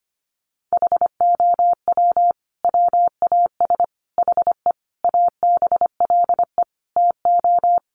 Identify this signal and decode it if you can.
Morse code, heard as a keyed tone, about 25 wpm, 700 Hz. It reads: HOW WAS HE ABLE TO
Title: Sentences from Top 1400 Words 25wpm